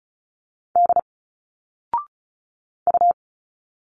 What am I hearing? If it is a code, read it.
Morse code, heard as a keyed tone, about 35 wpm, 700 Hz. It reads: D U
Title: Single Letters 35wpm